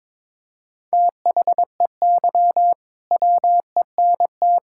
Morse code, heard as a keyed tone, about 21 wpm, 700 Hz. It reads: THEY WENT
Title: Sentences from Top 1600 Words 22wpm